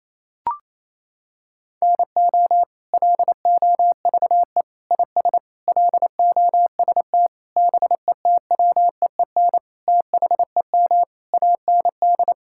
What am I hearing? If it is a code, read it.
Morse code, heard as a keyed tone, about 28 wpm, 700 Hz. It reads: NO LOVE IS LOST BETWEEN THEM AND
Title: Sentences from Top 1600 Words 28wpm